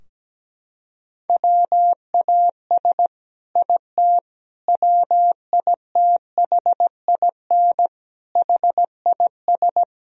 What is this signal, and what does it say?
Morse code, heard as a keyed tone, about 17 wpm, 700 Hz. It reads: WAS IT WITHIN HIS
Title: Sentences from Top 1000 Words - Review Part 4 - 17wpm